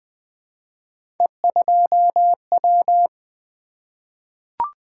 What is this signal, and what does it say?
Morse code, heard as a keyed tone, about 20 wpm, 700 Hz. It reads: E2W
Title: Three Letter-Number Combinations - 20wpm